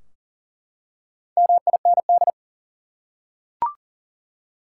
Morse code, heard as a keyed tone, about 40 wpm, 700 Hz. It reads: MIND
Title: Four Letter Words - 40wpm